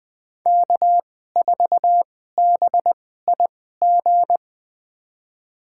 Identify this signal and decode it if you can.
Morse code, heard as a keyed tone, about 20 wpm, spaced as for 14 wpm, 700 Hz. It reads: K4BIG
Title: USA Call Signs - 5 Characters - 2x Character Spacing - 20wpm